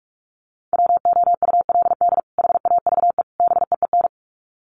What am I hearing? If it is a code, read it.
Morse code, heard as a keyed tone, about 44 wpm, 700 Hz. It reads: WOULD HAVE BEEN
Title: Sets of 3 Words 45wpm